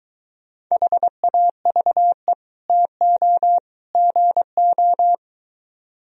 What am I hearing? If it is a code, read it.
Morse code, heard as a keyed tone, about 23 wpm, 700 Hz. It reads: HAVE TO GO